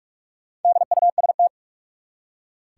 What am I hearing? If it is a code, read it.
Morse code, heard as a keyed tone, about 46 wpm, 700 Hz. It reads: DUST
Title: Four Letter Words - 45wpm